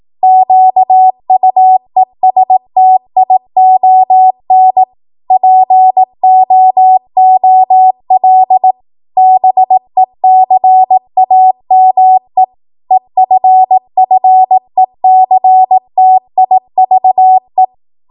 Morse code, heard as a keyed tone, about 18 wpm, 750 Hz. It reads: QUESTION POOL BECAME EFFECTIVE